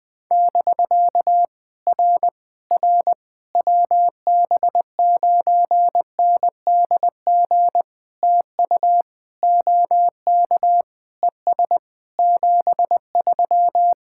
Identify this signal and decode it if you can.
Morse code, heard as a keyed tone, about 20 wpm, 700 Hz. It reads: <BK> R R WB9NDG TU OK ES 73
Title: POTA QSO Practice - 20wpm